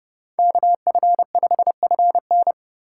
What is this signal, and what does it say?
Morse code, heard as a keyed tone, about 31 wpm, 700 Hz. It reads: KF5FN